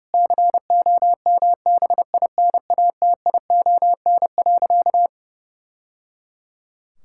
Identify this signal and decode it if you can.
Morse code, heard as a keyed tone, about 30 wpm, 700 Hz. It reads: COMBINATION.
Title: Sentences from Top 1000 Words - Review Part 6 - 30wpm